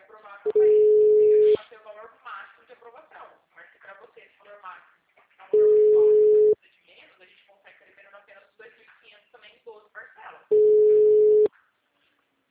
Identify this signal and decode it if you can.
Ringback tone (Europe, 425 Hz), with speech.
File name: out- -2006- - - .284